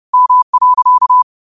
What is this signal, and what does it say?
Morse code, heard as a keyed tone, about 30 wpm, 1000 Hz. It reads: M.
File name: qotdfiles / 30wpm